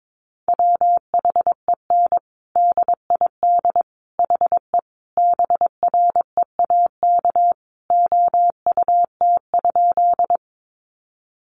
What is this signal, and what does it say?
Morse code, heard as a keyed tone, about 22 wpm, 700 Hz. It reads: WHEN DID HE BREAK OUT?
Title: Sentences from Top 900 Words 22wpm